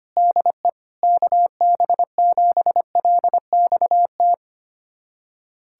Morse code, heard as a keyed tone, about 25 wpm, 700 Hz. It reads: DE KB7LXT
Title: USA Callsign DE Callsign - 6 Characters - 25wpm